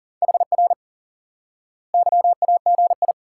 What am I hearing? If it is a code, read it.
Morse code, heard as a keyed tone, about 40 wpm, 700 Hz. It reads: HR YAGI
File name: IQE Course 3 - Lesson 89 - YAGI - Two Elements - 40wpm